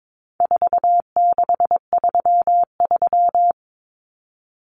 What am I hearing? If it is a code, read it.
Morse code, heard as a keyed tone, about 22 wpm, 700 Hz. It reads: 4633
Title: Common Cut Numbers - 4-Digit 22wpm